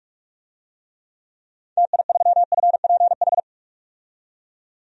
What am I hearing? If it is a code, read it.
Morse code, heard as a keyed tone, about 42 wpm, 700 Hz. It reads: TI3FPH
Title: DXCC-100 Call Signs - 6 Characters - 45wpm